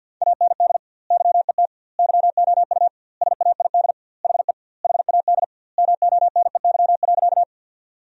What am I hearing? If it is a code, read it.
Morse code, heard as a keyed tone, about 47 wpm, 700 Hz. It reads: AND YET YOU SAID HE HAD MONEY.
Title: 2x Word-Spacing - Sentences from Top 500 Words - 50wpm